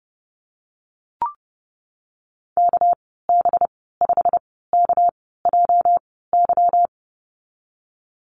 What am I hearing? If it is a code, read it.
Morse code, heard as a keyed tone, about 32 wpm, spaced as for 17 wpm, 700 Hz. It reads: KB5KJY
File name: Lesson 37g - Letter x - Call Signs 30wpm